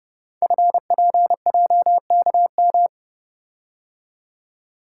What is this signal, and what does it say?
Morse code, heard as a keyed tone, about 30 wpm, 700 Hz. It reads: FPJKM
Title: Code Groups - 5 Letter:Numbers - 30wpm